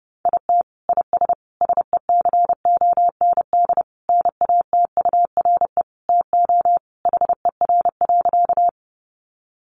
Morse code, heard as a keyed tone, about 30 wpm, 700 Hz. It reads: IT IS SECOND NATURE TO HER.
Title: Sentences from Top 600 Words 30wpm